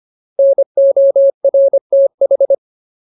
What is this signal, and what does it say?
Morse code, heard as a keyed tone, about 25 wpm, 550 Hz. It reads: NORTH
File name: Ultimate Word List / 25WPM Ultimate